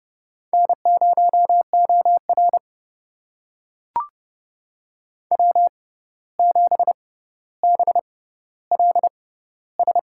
Morse code, heard as a keyed tone, about 30 wpm, spaced as for 14 wpm, 700 Hz. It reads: N0OR W7BLS
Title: USA Call Signs - 6x Character Spacing - 30wpm